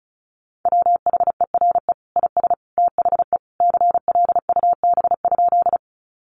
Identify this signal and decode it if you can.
Morse code, heard as a keyed tone, about 34 wpm, 700 Hz. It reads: WHERE IS THE CLUB?
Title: Sentences from Top 1200 Words 35wpm